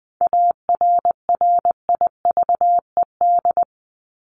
Morse code, heard as a keyed tone, about 20 wpm, 700 Hz. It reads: ARRIVED
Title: ED Suffix - Top 100 Words with Base Words 20wpm